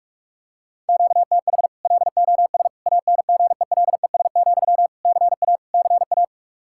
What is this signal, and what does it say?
Morse code, heard as a keyed tone, about 44 wpm, 700 Hz. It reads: QTH LOS ANGELES, CA CA
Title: Practice QSO - Exchange 1 - Beginning - 45wpm